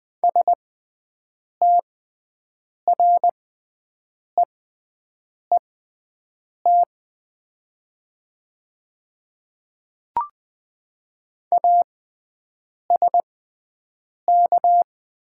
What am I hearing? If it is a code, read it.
Morse code, heard as a keyed tone, about 20 wpm, spaced as for 7 wpm, 700 Hz. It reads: STREET ASK
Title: 6x Character Spacing - Top 500 Words - 20wpm